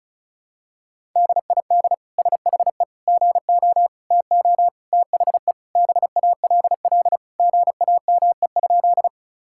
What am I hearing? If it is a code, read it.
Morse code, heard as a keyed tone, about 36 wpm, 700 Hz. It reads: DID SHE GO TO THE BALL GAME?
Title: Sentences from Top 1000 Words - Review Part 6 - 35wpm